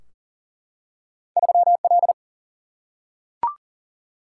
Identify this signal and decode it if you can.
Morse code, heard as a keyed tone, about 40 wpm, 700 Hz. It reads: 3 L
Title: Single-Letter Single-Number 40wpm